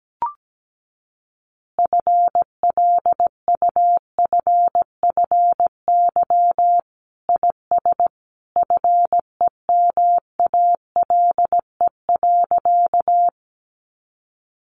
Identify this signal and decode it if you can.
Morse code, heard as a keyed tone, about 17 wpm, 700 Hz. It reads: FLUFFY IS FEMALE.
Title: Easy Phrases 17wpm